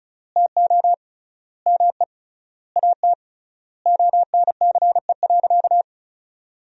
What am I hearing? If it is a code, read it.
Morse code, heard as a keyed tone, about 36 wpm, 700 Hz. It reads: TO ME AT ONCE.
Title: 3x Word-Spacing - Sentences from Top 500 Words - 35wpm